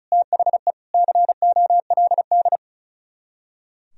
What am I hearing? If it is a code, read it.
Morse code, heard as a keyed tone, about 42 wpm, 700 Hz. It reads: THE COLD
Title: Speed-Racing - Sets of 5 Words 35wpm